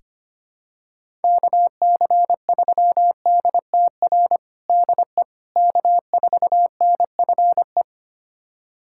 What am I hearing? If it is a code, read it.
Morse code, heard as a keyed tone, about 25 wpm, 700 Hz. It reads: KC3DTR DE K4NFE
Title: USA Callsign DE Callsign 25wpm